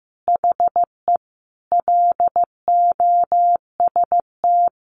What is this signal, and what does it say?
Morse code, heard as a keyed tone, about 15 wpm, 700 Hz. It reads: HE LOST
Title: Sentences from Top 1400 Words 15wpm